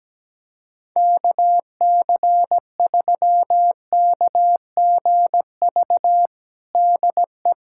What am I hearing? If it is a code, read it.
Morse code, heard as a keyed tone, about 17 wpm, 700 Hz. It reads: KC3KGV DE
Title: USA Callsign DE Callsign 17wpm